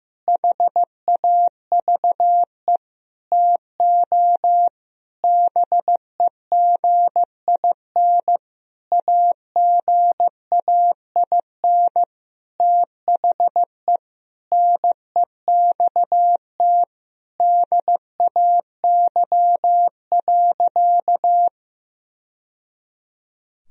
Morse code, heard as a keyed tone, about 16 wpm, 700 Hz. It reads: HAVE TO BEGIN AGAIN THE NEXT DAY.
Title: Sentences from Top 1000 Words - Review Part 1 - 15wpm